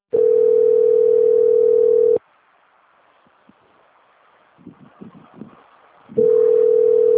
Ringback tone (North America), 440 + 480 Hz.